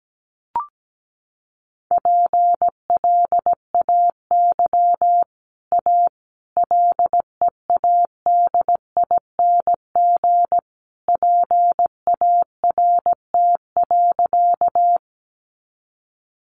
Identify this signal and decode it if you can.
Morse code, heard as a keyed tone, about 17 wpm, 700 Hz. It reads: PLAY A LEADING PART.
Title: Sentences from Top 1100 Words 17wpm